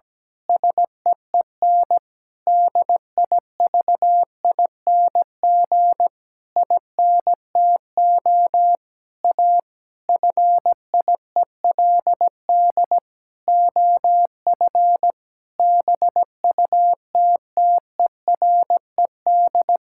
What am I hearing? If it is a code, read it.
Morse code, heard as a keyed tone, about 17 wpm, 700 Hz. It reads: SEEN DIVING INTO A FIELD OF BUTTERED
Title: Pangram Challenge v2 - 17wpm